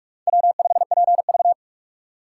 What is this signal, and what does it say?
Morse code, heard as a keyed tone, about 42 wpm, 700 Hz. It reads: W5PV